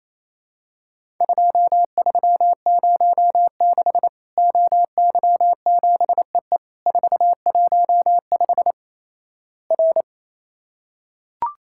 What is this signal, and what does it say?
Morse code, heard as a keyed tone, about 28 wpm, 700 Hz. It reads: 2306 OY7EE 415 R
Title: Worked All Europe DX Contest - QTC Exchange - 28wpm